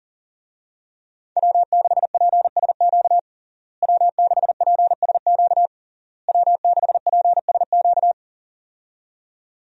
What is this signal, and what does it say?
Morse code, heard as a keyed tone, about 40 wpm, 700 Hz. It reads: W6PSQ W6PSQ W6PSQ
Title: USA Call Signs - 5 Characters - Repeated 3x - 40wpm